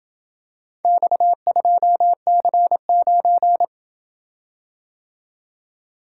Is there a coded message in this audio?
Yes: Morse code from 0.8 to 3.6 s, heard as a keyed tone, about 26 wpm, 700 Hz. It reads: X2C9